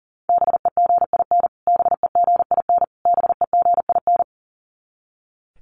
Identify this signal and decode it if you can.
Morse code, heard as a keyed tone, about 40 wpm, 700 Hz. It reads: BEGIN BEGIN BEGIN
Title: Top 500 Words Repeated 40wpm